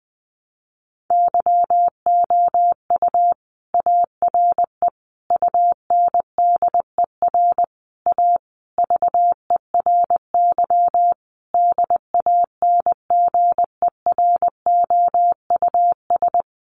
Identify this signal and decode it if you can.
Morse code, heard as a keyed tone, about 20 wpm, 700 Hz. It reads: YOU ARE UNDER A VERY DANGEROUS
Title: Sentences from Top 1200 Words 20wpm